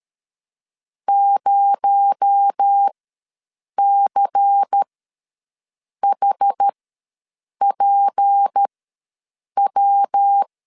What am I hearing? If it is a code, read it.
Morse code, heard as a keyed tone, about 13 wpm, spaced as for 7 wpm, 800 Hz. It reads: 0CHPW
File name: LEZIONI / 10 SIGNS / 10A SLASH ws 12wpm 3443